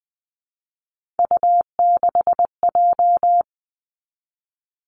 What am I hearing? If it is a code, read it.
Morse code, heard as a keyed tone, about 20 wpm, 700 Hz. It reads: U6J
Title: Three Challenging Letter-Number Combinations 20wpm